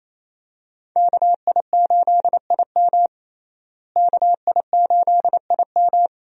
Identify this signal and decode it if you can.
Morse code, heard as a keyed tone, about 28 wpm, 700 Hz. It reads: KI8IM KI8IM
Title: USA Call Signs - 5 Characters - Repeated 2x - 28wpm